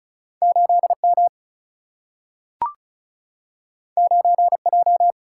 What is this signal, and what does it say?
Morse code, heard as a keyed tone, about 36 wpm, 700 Hz. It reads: 8 M 9 J
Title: Single-Letter Single-Number 35wpm